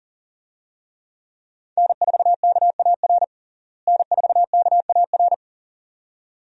Morse code, heard as a keyed tone, about 40 wpm, 700 Hz. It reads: N4KAR N4KAR
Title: USA Call Signs - 5 Characters - Repeated 2x - 40wpm